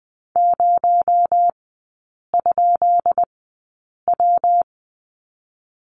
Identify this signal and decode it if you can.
Morse code, heard as a keyed tone, about 20 wpm, spaced as for 8 wpm, 700 Hz. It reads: 0?W
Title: 218 Lesson 35e - Number 0 - Three Characters 20wpm